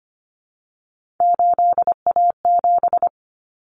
Morse code, heard as a keyed tone, about 25 wpm, 700 Hz. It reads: 8A7
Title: Common Cut Numbers - 3-Digit 25wpm